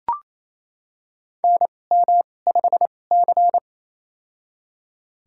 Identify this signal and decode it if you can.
Morse code, heard as a keyed tone, about 28 wpm, spaced as for 20 wpm, 700 Hz. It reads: NM5C
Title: USA Call Signs - 4 Characters - 2x Character Spacing - 28wpm